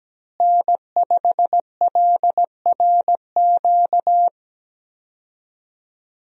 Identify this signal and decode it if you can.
Morse code, heard as a keyed tone, about 17 wpm, 700 Hz. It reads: N5LRQ
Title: USA Call Signs - 5 Characters - Repeated 2x - 17wpm